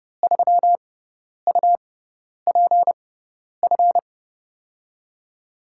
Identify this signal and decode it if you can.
Morse code, heard as a keyed tone, about 30 wpm, spaced as for 10 wpm, 700 Hz. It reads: 3UPF
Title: USA Call Signs - 6 Characters - 6x Character Spacing - 30wpm